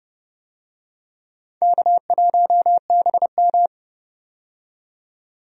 Morse code, heard as a keyed tone, about 31 wpm, 700 Hz. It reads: K1BM
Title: USA Call Signs - 4 Characters - 30wpm